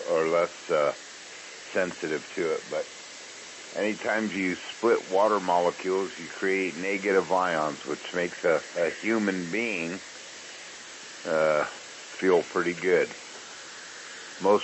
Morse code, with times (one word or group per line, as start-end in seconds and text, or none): none